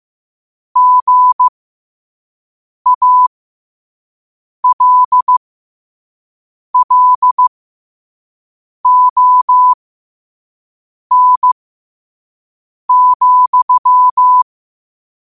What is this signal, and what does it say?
Morse code, heard as a keyed tone, about 15 wpm, spaced as for 5 wpm, 1000 Hz. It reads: GALLON,
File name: qotdfiles / 5wpm